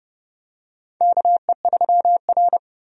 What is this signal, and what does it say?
Morse code, heard as a keyed tone, about 32 wpm, 700 Hz. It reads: KE3R
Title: USA Call Signs - 4 Characters - 30wpm